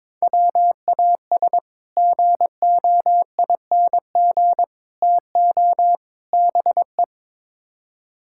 Morse code, heard as a keyed tone, about 22 wpm, 700 Hz. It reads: WAS GOING TO BE
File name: practice / 22wpm / Build A Phrase - 5-Word Phrases - 22wpm